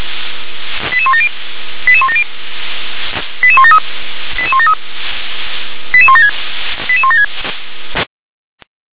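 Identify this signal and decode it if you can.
Five-tone selective call (ZVEI), heard as quick runs of five tones, tones of about 70 ms. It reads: 88170 88170 88173 88173 88176 88176
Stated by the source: ZVEI five-tone call received by SDR.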